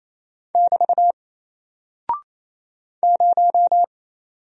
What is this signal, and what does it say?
Morse code, heard as a keyed tone, about 28 wpm, 700 Hz. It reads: <BT> 0
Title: Single Character (Letter:Number:Punctuation:Prosign) Alt 28wpm